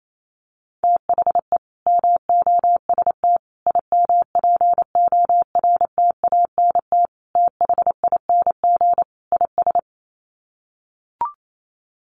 Morse code, heard as a keyed tone, about 28 wpm, 700 Hz. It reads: THE MOST IMPORTANT THING IS
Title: Sets of 5 Words Encore 28wpm